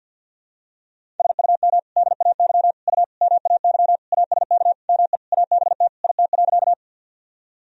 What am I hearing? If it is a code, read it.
Morse code, heard as a keyed tone, about 52 wpm, 700 Hz. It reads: SUM DAY U MAY ASK ME ABT IT.
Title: Sentences from Top 200 Words and Abbreviations 50wpm